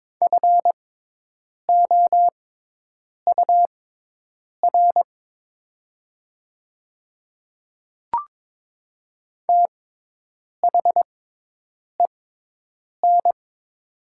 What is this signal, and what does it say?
Morse code, heard as a keyed tone, about 22 wpm, spaced as for 8 wpm, 700 Hz. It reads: FOUR THEN
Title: 6x Character Spacing - Top 300 Words - 22wpm